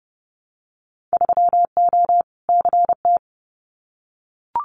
Morse code, heard as a keyed tone, about 30 wpm, 700 Hz. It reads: 3O CT
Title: Winter Field Day Exchanges 30wpm